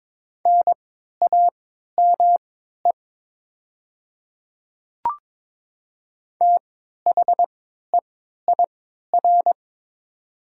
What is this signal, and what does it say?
Morse code, heard as a keyed tone, about 22 wpm, spaced as for 12 wpm, 700 Hz. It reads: NAME THEIR
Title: 3x Character Spacing - Top 200 Words - 22wpm